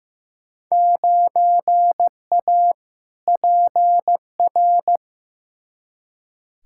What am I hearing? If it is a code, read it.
Morse code, heard as a keyed tone, about 15 wpm, 700 Hz. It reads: 9A PR
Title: Field Day Exchanges Encore - 15wpm